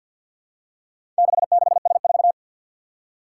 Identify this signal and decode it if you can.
Morse code, heard as a keyed tone, about 47 wpm, 700 Hz. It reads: 66S4
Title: Four Challenging Letter-Number Combinations 50wpm